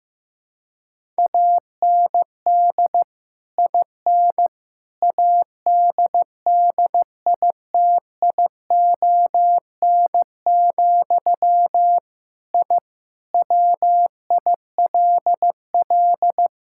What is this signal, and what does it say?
Morse code, heard as a keyed tone, about 15 wpm, 700 Hz. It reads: AND IN ADDITION, I WILL
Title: Sentences from Top 1000 Words - Review Part 6 - 15wpm